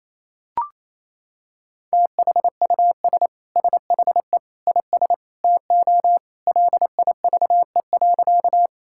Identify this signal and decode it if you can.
Morse code, heard as a keyed tone, about 27 wpm, 700 Hz. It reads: THUS SHE IS TO LIVE.
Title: Sentences from Top 1000 Words - Review Part 5 - 28wpm